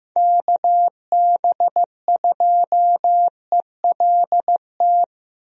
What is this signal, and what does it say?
Morse code, heard as a keyed tone, about 15 wpm, 700 Hz. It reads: KB2ELT
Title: USA Call Signs - 6 Characters - Repeated 2x - 15wpm